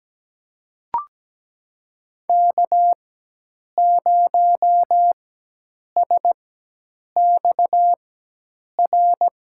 Morse code, heard as a keyed tone, about 17 wpm, spaced as for 8 wpm, 700 Hz. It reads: K0SXR
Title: USA Call Signs - 5 Characters - 4x Character Spacing - 17wpm